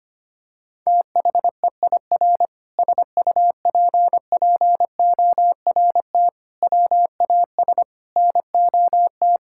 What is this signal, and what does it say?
Morse code, heard as a keyed tone, about 25 wpm, 700 Hz. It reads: THEIR SUPPORT WAS NOT
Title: Sentences from Top 1000 Words - Review Part 6 - 25wpm